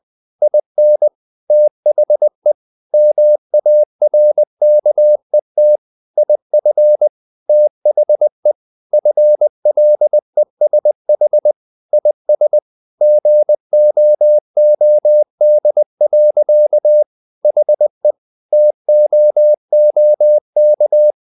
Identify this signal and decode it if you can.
Morse code, heard as a keyed tone, about 20 wpm, 600 Hz. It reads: IN THE MARKET IF THE FLESH IS GOOD. HE TOOK
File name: the old man and the sea / the old man and the sea part 1 20wpm